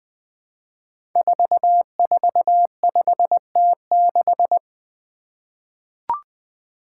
Morse code, heard as a keyed tone, about 20 wpm, 700 Hz. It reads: 445T6
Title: Five Challenging Letter-Number Combinations 20wpm